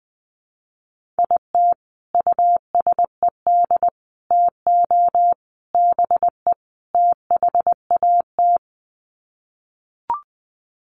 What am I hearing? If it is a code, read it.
Morse code, heard as a keyed tone, about 20 wpm, 700 Hz. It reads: IT USED TO BE THAT
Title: Sets of 5 Words Encore2 20wpm